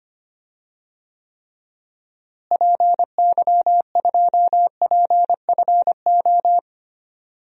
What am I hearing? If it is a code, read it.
Morse code, heard as a keyed tone, about 25 wpm, 700 Hz. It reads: PY2PFO
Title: DXCC-100 Call Signs - 6 Characters - Repeated 2x - 25wpm